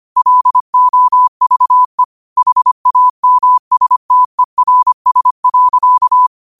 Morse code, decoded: LOVE HAMSTERS.

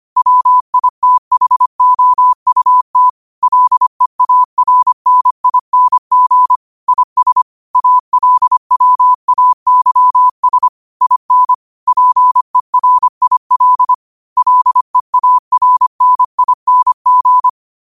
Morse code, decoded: WITHOUT LEARNING IS ALWAYS IN PERIL LEARNING

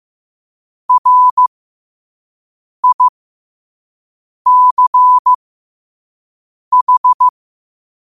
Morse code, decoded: RICH

